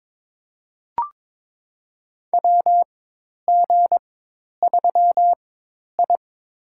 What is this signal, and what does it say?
Morse code, heard as a keyed tone, about 22 wpm, spaced as for 10 wpm, 700 Hz. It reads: WG3I
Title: USA Call Signs - 4x Character Spacing - 22wpm